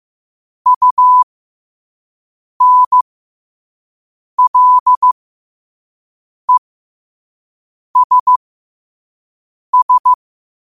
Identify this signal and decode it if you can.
Morse code, heard as a keyed tone, about 15 wpm, spaced as for 5 wpm, 1000 Hz. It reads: UNLESS
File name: qotdfiles / 5wpm